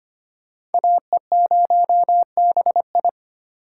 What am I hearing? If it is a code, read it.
Morse code, heard as a keyed tone, about 34 wpm, spaced as for 30 wpm, 700 Hz. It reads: AE0BI
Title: Speed Racing - USA 2x2 Call Signs 25wpm